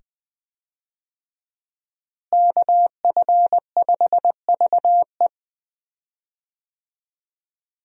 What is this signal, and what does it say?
Morse code, heard as a keyed tone, about 20 wpm, 700 Hz. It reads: KF5VE
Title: USA Call Signs - 5 Characters - Repeated 2x - 20wpm